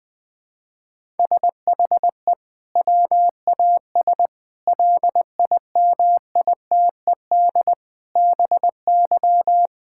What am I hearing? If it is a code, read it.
Morse code, heard as a keyed tone, about 20 wpm, 700 Hz. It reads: SHE WAS LIMITED BY
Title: Sentences from Top 1000 Words - Review Part 6 - 20wpm